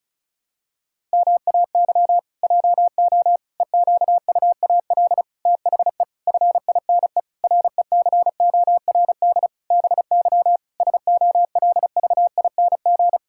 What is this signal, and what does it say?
Morse code, heard as a keyed tone, about 34 wpm, 700 Hz. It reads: MAY JO EQUAL THE FINE RECORD BY SOLVING
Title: Pangram Challenge v2 - 35wpm